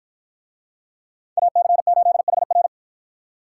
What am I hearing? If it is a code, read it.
Morse code, heard as a keyed tone, about 51 wpm, 700 Hz. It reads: AC8HR